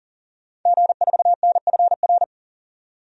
Morse code, heard as a keyed tone, about 40 wpm, 700 Hz. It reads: G4NFR
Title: DXCC-100 Call Signs - 5 Characters - 40wpm